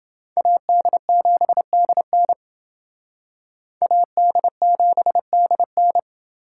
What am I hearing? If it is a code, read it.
Morse code, heard as a keyed tone, about 28 wpm, 700 Hz. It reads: AD7DN AD7DN